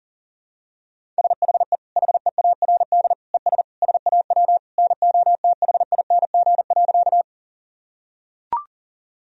Morse code, heard as a keyed tone, about 40 wpm, 700 Hz. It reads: SHE HEARD ES SAW NOTHING.
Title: Sentences from Top 500 Words and Abbreviations 40wpm